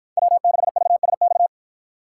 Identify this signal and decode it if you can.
Morse code, heard as a keyed tone, about 53 wpm, 700 Hz. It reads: W6VSX